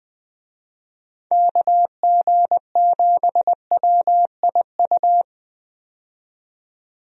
Morse code, heard as a keyed tone, about 20 wpm, 700 Hz. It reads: KG7WIU
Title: USA Call Signs - 6 Characters - 20wpm